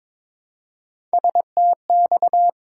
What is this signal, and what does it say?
Morse code, heard as a keyed tone, about 22 wpm, 700 Hz. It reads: STX
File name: ARRL Sections v2 - 22wpm